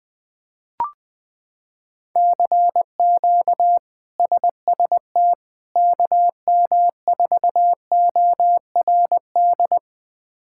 Morse code, heard as a keyed tone, about 21 wpm, 700 Hz. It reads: CQ SST KM4ORD
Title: K1USN SST QSO Practice - 20wpm